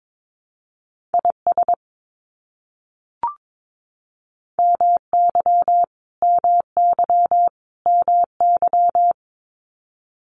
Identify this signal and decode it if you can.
Morse code, heard as a keyed tone, about 22 wpm, 700 Hz. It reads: IS MY MY MY
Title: Top 500 Words Repeated 22wpm